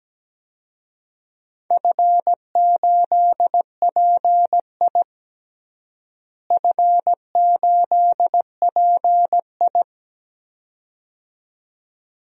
Morse code, heard as a keyed tone, about 17 wpm, 700 Hz. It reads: F8PI F8PI
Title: DXCC-100 Call Signs - 4 Characters - Repeated 2x - 17wpm